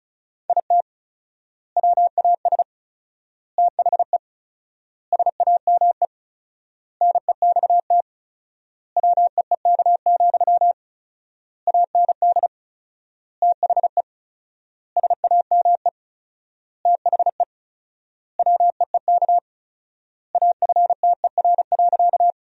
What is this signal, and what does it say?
Morse code, heard as a keyed tone, about 36 wpm, 700 Hz. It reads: IT WAS THE SAME NEXT WEEK, AND THE SAME THE WEEK AFTER.
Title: 4x Word-Spacing - Sentences from Top 500 Words - 35wpm